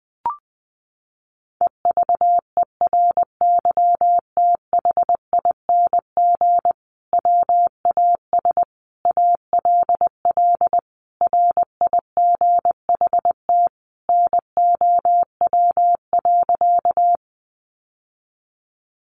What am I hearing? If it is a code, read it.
Morse code, heard as a keyed tone, about 20 wpm, 700 Hz. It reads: EVERYTHING WAS ALL RIGHT NOW.
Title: Sentences from Top 700 Words 20wpm